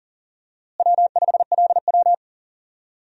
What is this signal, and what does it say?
Morse code, heard as a keyed tone, about 40 wpm, 700 Hz. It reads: W5LW
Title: USA Call Signs - 4 Characters - Repeated 3x - 40wpm